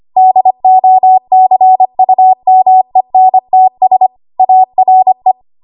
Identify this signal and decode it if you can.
Morse code, heard as a keyed tone, about 25 wpm, 750 Hz. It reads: DOCUMENTS ARE